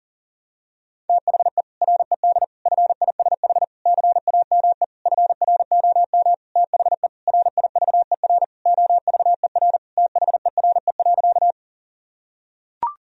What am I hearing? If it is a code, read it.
Morse code, heard as a keyed tone, about 40 wpm, 700 Hz. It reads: THE RED FISH CAME FROM THE RIVER OVER THERE.
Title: Sentences from Top 1000 Words - Review Part 2 - 40wpm